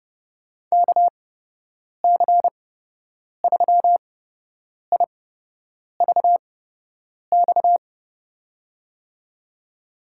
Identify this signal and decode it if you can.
Morse code, heard as a keyed tone, about 32 wpm, spaced as for 8 wpm, 700 Hz. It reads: KC3IVX
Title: USA Call Signs - 6 Characters - 8x Character Spacing - 30wpm